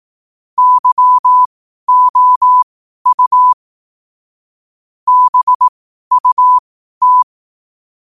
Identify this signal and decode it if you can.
Morse code, heard as a keyed tone, about 18 wpm, spaced as for 12 wpm, 1000 Hz. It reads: YOU BUT
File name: qotdfiles / 12wpm